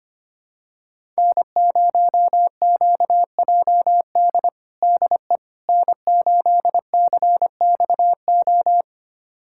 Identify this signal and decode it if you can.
Morse code, heard as a keyed tone, about 25 wpm, 700 Hz. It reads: N0QJD DE N8CXO
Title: USA Callsign DE Callsign - 5 Characters - 25wpm